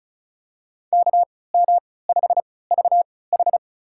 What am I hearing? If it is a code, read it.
Morse code, heard as a keyed tone, about 34 wpm, spaced as for 20 wpm, 700 Hz. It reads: KM5VH
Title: USA Call Signs - 5 Characters - 3x Character Spacing - 35wpm